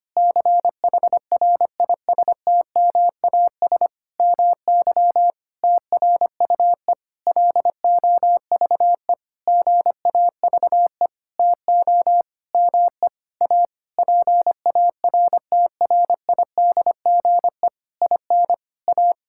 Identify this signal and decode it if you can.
Morse code, heard as a keyed tone, about 25 wpm, 700 Hz. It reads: CHRISTMAS MY TRUE LOVE GAVE TO ME A PARTRIDGE IN A
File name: practice / 25wpm / Sentences with 9-Word Phrases - 25wpm